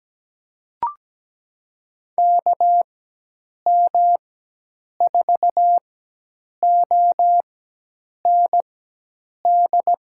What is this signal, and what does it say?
Morse code, heard as a keyed tone, about 17 wpm, spaced as for 8 wpm, 700 Hz. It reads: KM4OND
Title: USA Call Signs - 6 Characters - 4x Character Spacing - 17wpm